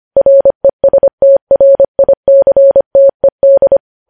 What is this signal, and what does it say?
Morse code, heard as a keyed tone, about 25 wpm, 550 Hz. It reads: RESTRICTED
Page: news headlines in morse code at 25 wpm